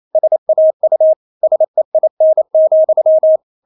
Morse code, heard as a keyed tone, about 28 wpm, 630 Hz. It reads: SAU SEIN,
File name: Witz9 28wpm